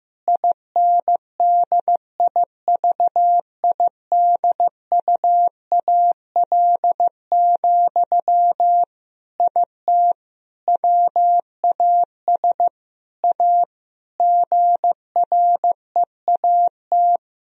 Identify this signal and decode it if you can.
Morse code, heard as a keyed tone, about 15 wpm, 700 Hz. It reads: INDIVIDUAL, IT WAS A GREAT
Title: Sentences from Top 1000 Words - Review Part 4 - 15wpm